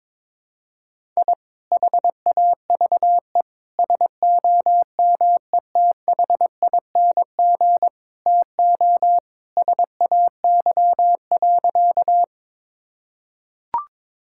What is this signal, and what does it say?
Morse code, heard as a keyed tone, about 22 wpm, 700 Hz. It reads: I HAVE SOMETHING TO SAY.
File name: practice / 22wpm / Sentences with 2-Word Phrases - 22wpm